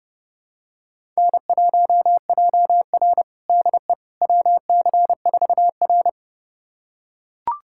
Morse code, heard as a keyed tone, about 32 wpm, 700 Hz. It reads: N1JR DE WC4R
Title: USA Callsign DE Callsign - 4 Characters - 30wpm